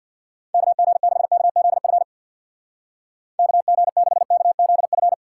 Morse code, heard as a keyed tone, about 64 wpm, 700 Hz. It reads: KG6KZF KG6KZF